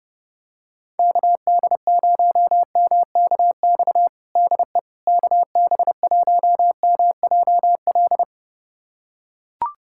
Morse code, heard as a keyed tone, about 30 wpm, 700 Hz. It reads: KD0MKX DE KB1MJL
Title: USA Callsign DE Callsign - 6 Characters - 30wpm